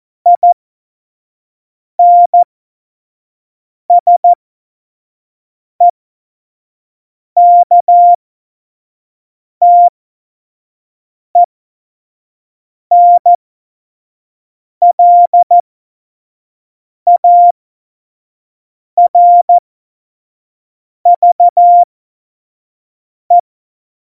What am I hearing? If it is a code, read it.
Morse code, heard as a keyed tone, about 14 wpm, spaced as for 5 wpm, 700 Hz. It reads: INSEKTENLARVE
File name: Morsekurs / Witze / Witz 30. Juli 2019, 25 F70